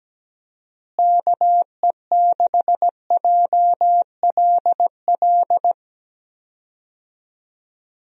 Morse code, heard as a keyed tone, about 17 wpm, 700 Hz. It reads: KE6JLL
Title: USA Call Signs - 6 Characters - Repeated 3x - 17wpm